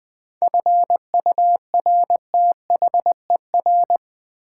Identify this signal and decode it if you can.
Morse code, heard as a keyed tone, about 20 wpm, 700 Hz. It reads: FURTHER
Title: 7 Letter Words - First 3 Letters Given - 20wpm